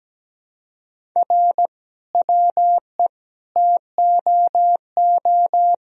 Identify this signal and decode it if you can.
Morse code, heard as a keyed tone, about 17 wpm, 700 Hz. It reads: R WE TOO
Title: Sentences from Top 300 Words and Abbreviations 17wpm